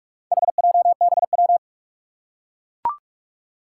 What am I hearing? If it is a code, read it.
Morse code, heard as a keyed tone, about 44 wpm, 700 Hz. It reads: HJBW
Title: Four Challenging Letter-Number Combinations 45wpm